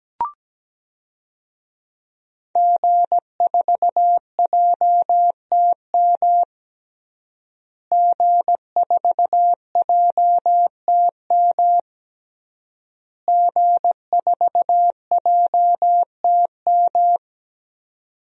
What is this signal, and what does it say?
Morse code, heard as a keyed tone, about 17 wpm, 700 Hz. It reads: G4JTM G4JTM G4JTM
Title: DXCC-100 Call Signs - 5 Characters - Repeated 3x - 17wpm